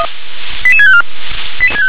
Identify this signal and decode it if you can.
Five-tone selective call (ZVEI), heard as quick runs of five tones, tones of about 70 ms. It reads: 88654